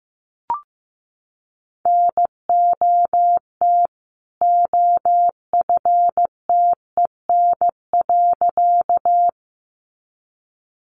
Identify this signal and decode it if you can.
Morse code, heard as a keyed tone, about 15 wpm, 700 Hz. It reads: NOT OFTEN.
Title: A Sound Of Thunder 15wpm